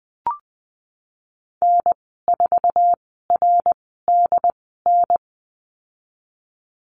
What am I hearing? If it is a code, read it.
Morse code, heard as a keyed tone, about 19 wpm, spaced as for 14 wpm, 700 Hz. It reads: N4RDN
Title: 076 Lesson 13g - Letter L - Call Signs 20wpm